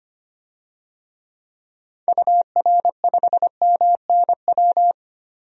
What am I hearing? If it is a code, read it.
Morse code, heard as a keyed tone, about 25 wpm, 700 Hz. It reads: UR5MNW
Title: DXCC-100 Call Signs - 6 Characters - 25wpm